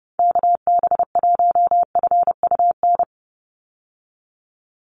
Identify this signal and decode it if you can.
Morse code, heard as a keyed tone, about 30 wpm, 700 Hz. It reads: KB1FUN
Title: USA Call Signs 30wpm